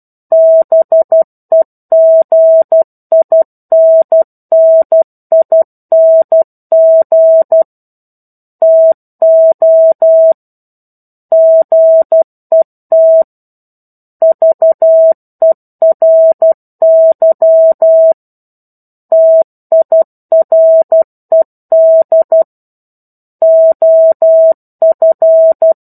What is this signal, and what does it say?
Morse code, heard as a keyed tone, about 12 wpm, 650 Hz. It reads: BEGINNING TO GET VERY TIRED OF
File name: 01-Chapter-01 12 WPM